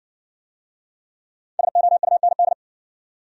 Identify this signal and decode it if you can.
Morse code, heard as a keyed tone, about 60 wpm, 700 Hz. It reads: SOUND